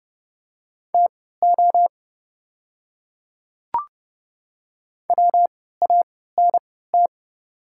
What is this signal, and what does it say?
Morse code, heard as a keyed tone, about 31 wpm, spaced as for 17 wpm, 700 Hz. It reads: TO WANT
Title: 3x Character Spacing - Top 200 Words - 30wpm